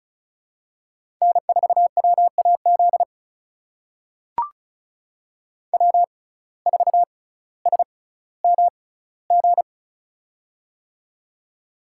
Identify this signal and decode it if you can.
Morse code, heard as a keyed tone, about 36 wpm, spaced as for 18 wpm, 700 Hz. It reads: N4WAZ W4SMG
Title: USA Call Signs - 5 Characters - 6x Character Spacing - 35wpm